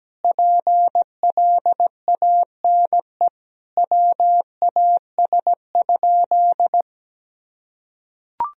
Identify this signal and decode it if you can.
Morse code, heard as a keyed tone, about 17 wpm, 700 Hz. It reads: PLANE WAS?
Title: Sentences from Top 1000 Words - Review Part 3 - 17wpm